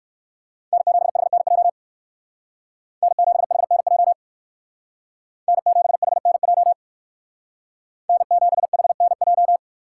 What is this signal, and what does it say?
Morse code, heard as a keyed tone, about 58 wpm, 700 Hz. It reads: N7HNJ N7HNJ N7HNJ N7HNJ